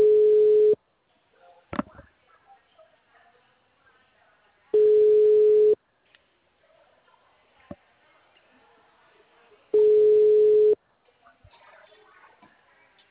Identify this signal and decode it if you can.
Ringback tone (Europe, 425 Hz).
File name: out- -2002